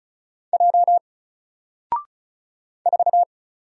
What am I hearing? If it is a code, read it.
Morse code, heard as a keyed tone, about 36 wpm, 700 Hz. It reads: J 4
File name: Single-Letter-Number-35wpm